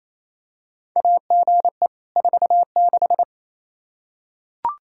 Morse code, heard as a keyed tone, about 28 wpm, 700 Hz. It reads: AGE 46
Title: Practice QSO - Exchange 3 - More Info - 28wpm